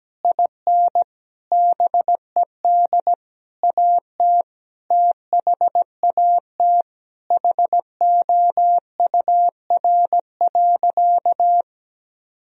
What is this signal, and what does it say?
Morse code, heard as a keyed tone, about 17 wpm, 700 Hz. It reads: IN BED AT THAT HOUR.
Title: Sentences from Top 1000 Words - Review Part 3 - 17wpm